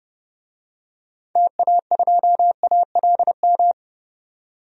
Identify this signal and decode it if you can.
Morse code, heard as a keyed tone, about 30 wpm, 700 Hz. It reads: TA2ALM
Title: DXCC-100 Call Signs - 6 Characters - 30wpm